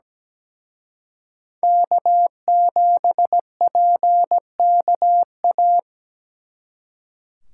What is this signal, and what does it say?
Morse code, heard as a keyed tone, about 20 wpm, 700 Hz. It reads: K7PKA
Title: Speed Racing - USA 1x3 Call Signs 17wpm